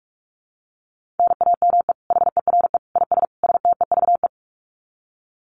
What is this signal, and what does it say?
Morse code, heard as a keyed tone, about 44 wpm, 700 Hz. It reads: NAME HERE IS STEVE
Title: Common QSO Phrases 45wpm